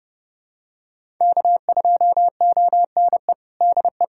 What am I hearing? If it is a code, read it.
Morse code, heard as a keyed tone, about 30 wpm, 700 Hz. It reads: K2ONE DE
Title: USA Callsign DE Callsign - 5 Characters - 30wpm